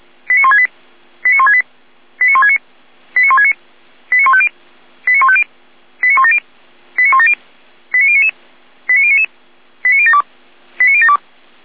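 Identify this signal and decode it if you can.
Five-tone selective call (ZVEI), heard as quick runs of five tones, tones of about 70 ms. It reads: 89168 89168 89169 89169 89150 89150 89170 89170 89099 89099 89072 89072
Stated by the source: ZVEI five-tone call received by SDR.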